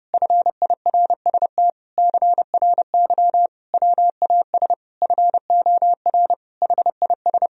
Morse code, heard as a keyed tone, about 30 wpm, 700 Hz. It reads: FIRST CRY WAS FOR HIS
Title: Sentences from Top 1000 Words - Review Part 2 - 30wpm